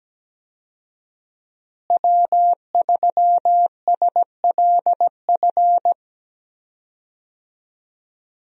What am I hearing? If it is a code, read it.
Morse code, heard as a keyed tone, about 17 wpm, 700 Hz. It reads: W3SLF
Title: USA Call Signs - 5 Characters - Repeated 3x - 17wpm